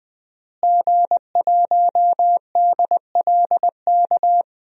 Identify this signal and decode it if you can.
Morse code, heard as a keyed tone, about 20 wpm, 700 Hz. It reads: G1DLK
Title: DXCC-100 Call Signs - 5 Characters - Repeated 2x - 20wpm